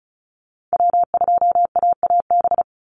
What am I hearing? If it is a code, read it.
Morse code, heard as a keyed tone, about 34 wpm, 700 Hz. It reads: W2AAB
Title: Call Signs Repeated 2x - 35wpm